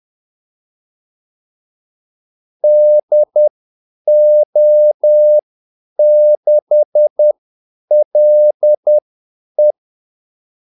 Morse code, heard as a keyed tone, about 10 wpm, spaced as for 8 wpm, 600 Hz. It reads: DO6LE
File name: Callsigns-10-8wpm